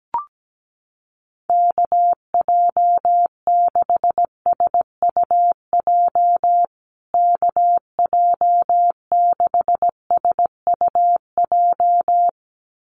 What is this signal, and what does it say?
Morse code, heard as a keyed tone, about 17 wpm, 700 Hz. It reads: KJ6SUJ KJ6SUJ
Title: USA Call Signs Repeated 2x 17wpm